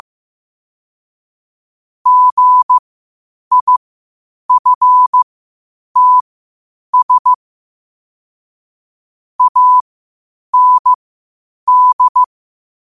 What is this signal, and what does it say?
Morse code, heard as a keyed tone, about 15 wpm, spaced as for 8 wpm, 1000 Hz. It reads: GIFTS AND
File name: qotdfiles / 8wpm